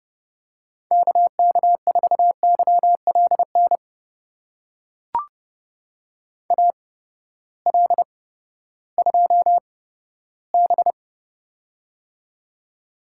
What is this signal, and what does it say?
Morse code, heard as a keyed tone, about 32 wpm, spaced as for 15 wpm, 700 Hz. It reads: KK4YLN AL2B
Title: USA Call Signs - 8x Character Spacing - 30wpm